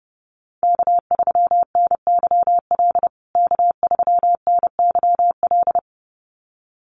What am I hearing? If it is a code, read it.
Morse code, heard as a keyed tone, about 30 wpm, 700 Hz. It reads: K3NYL K3NYL
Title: USA Call Signs Repeated 2x 30wpm